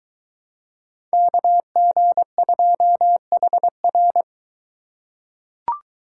Morse code, heard as a keyed tone, about 21 wpm, 700 Hz. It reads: KG2HR